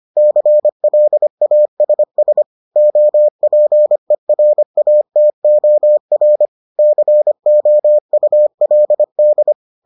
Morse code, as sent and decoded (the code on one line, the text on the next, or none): -.-. .-.. .- ... ... / --- .--. .-. .- - --- .-. / -.-. --- ..- .-.. -..
CLASS OPERATOR COULD